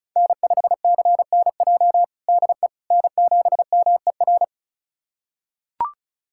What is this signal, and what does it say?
Morse code, heard as a keyed tone, about 34 wpm, 700 Hz. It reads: N5CNJ DE N7MER
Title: USA Callsign DE Callsign - 5 Characters - 35wpm